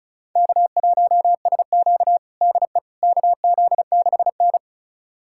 Morse code, heard as a keyed tone, about 34 wpm, 690 Hz. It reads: K1SQ DE KZ6N
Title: USA Callsign DE Callsign - 4 Characters - 35wpm